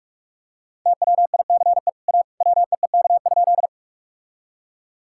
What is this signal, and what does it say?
Morse code, heard as a keyed tone, about 46 wpm, 700 Hz. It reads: TWICE A WEEK?
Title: Sentences from Top 1600 Words 45wpm